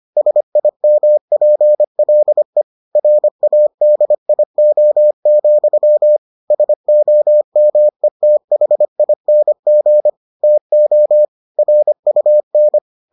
Morse code, heard as a keyed tone, about 25 wpm, 600 Hz. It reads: SIMPLE RADIO, SOMETHING TO RUN